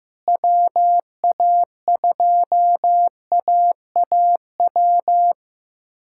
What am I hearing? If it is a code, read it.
Morse code, heard as a keyed tone, about 15 wpm, 700 Hz. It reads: WA2AAW
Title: K1USN SST QSO Practice - 15wpm